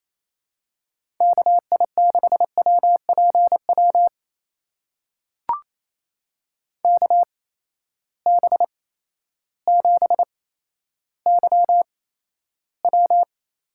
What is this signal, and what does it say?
Morse code, heard as a keyed tone, about 28 wpm, spaced as for 13 wpm, 700 Hz. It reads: KI6WPW KB7YW
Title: USA Call Signs - 6 Characters - 8x Character Spacing - 28wpm